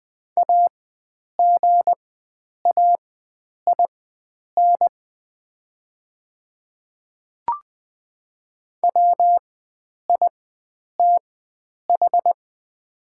Morse code, heard as a keyed tone, about 20 wpm, spaced as for 9 wpm, 700 Hz. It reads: AGAIN WITH
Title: 4x Character Spacing - Top 300 Words - 20wpm